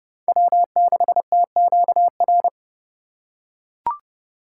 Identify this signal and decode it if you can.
Morse code, heard as a keyed tone, about 29 wpm, 700 Hz. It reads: W6TQR